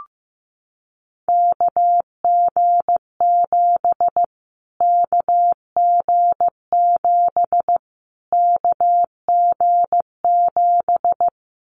Morse code, heard as a keyed tone, about 15 wpm, 700 Hz. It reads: KG7 KG7 KG7
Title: Callsign Prefixes for Top 25 DXCC - Repeated 3x - 15wpm